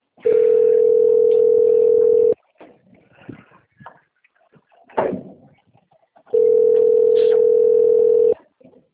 Ringback tone (North America), 440 + 480 Hz.
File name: out- - - - - .2747